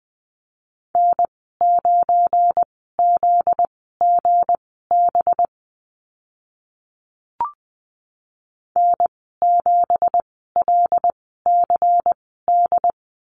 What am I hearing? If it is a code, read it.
Morse code, heard as a keyed tone, about 19 wpm, spaced as for 14 wpm, 700 Hz. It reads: N9ZGB N7LCD
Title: 245 Lesson 39g - Letter z - Call Signs 20wpm